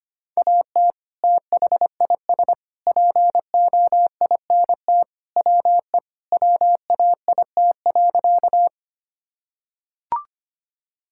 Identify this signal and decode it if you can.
Morse code, heard as a keyed tone, about 25 wpm, 700 Hz. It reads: AT THIS POINT WE WAIT.
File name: practice / 25wpm / Sentences with 3-Word Phrases - 25wpm